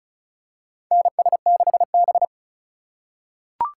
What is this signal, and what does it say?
Morse code, heard as a keyed tone, about 36 wpm, 690 Hz. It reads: NS6B
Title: USA Call Signs - 4 Characters - 2x Character Spacing - 35wpm